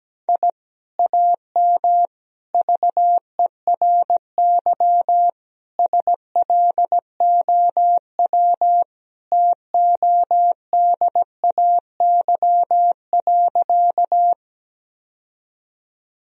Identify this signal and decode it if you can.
Morse code, heard as a keyed tone, about 17 wpm, 700 Hz. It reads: I AM VERY SLOW TODAY.
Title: Sentences from Top 1000 Words - Review Part 2 - 17wpm